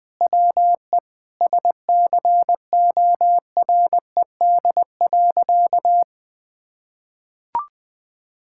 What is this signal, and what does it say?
Morse code, heard as a keyed tone, about 20 wpm, 700 Hz. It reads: WE SCORED.
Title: Sentences with 6-Word Phrases - 20wpm